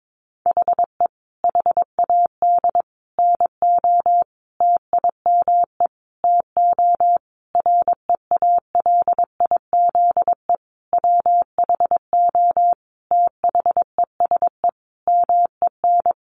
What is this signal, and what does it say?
Morse code, heard as a keyed tone, about 22 wpm, 700 Hz. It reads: HE HAD NO TIME TO REALIZE WHO THESE MEN
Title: Sentences from Top 1000 Words 22wpm